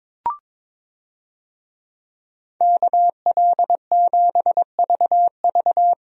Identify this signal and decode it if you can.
Morse code, heard as a keyed tone, about 22 wpm, 700 Hz. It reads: KL7VV
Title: USA Call Signs - 5 Characters - 22wpm